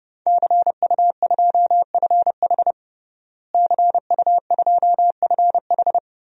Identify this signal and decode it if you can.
Morse code, heard as a keyed tone, about 30 wpm, 700 Hz. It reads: CU2FH CU2FH
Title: DXCC-100 Call Signs - 5 Characters - Repeated 3x - 30wpm